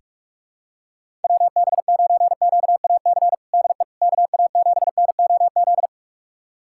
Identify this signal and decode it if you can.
Morse code, heard as a keyed tone, about 43 wpm, 700 Hz. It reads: WB9QAC DE KA7NOZ
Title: USA Callsign DE Callsign - 6 Characters - 45wpm